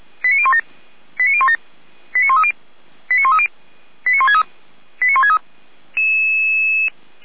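Five-tone selective call (ZVEI), heard as quick runs of five tones, tones of about 70 ms. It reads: 89017 89017 89130 89130 89163 89163
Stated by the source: ZVEI five-tone selective call received via SDR, from the Feuerwehr Orlamünde fire brigade audio archive.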